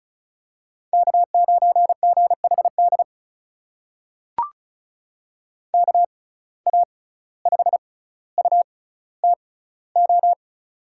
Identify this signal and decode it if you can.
Morse code, heard as a keyed tone, about 36 wpm, spaced as for 17 wpm, 700 Hz. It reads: K9GHD KA5UTO
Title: USA Call Signs - 6x Character Spacing - 35wpm